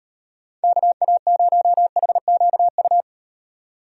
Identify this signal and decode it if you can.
Morse code, heard as a keyed tone, about 38 wpm, 700 Hz. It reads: KA0HQU